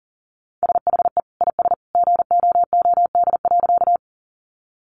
Morse code, heard as a keyed tone, about 40 wpm, 700 Hz. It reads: SHE IS GOOD.
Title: Easy Phrases 40wpm